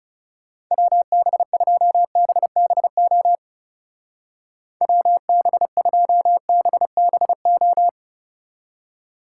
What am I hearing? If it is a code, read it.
Morse code, heard as a keyed tone, about 36 wpm, 700 Hz. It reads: WB2BBO WB2BBO
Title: Speed Racing - USA 2x3 Call Signs 30wpm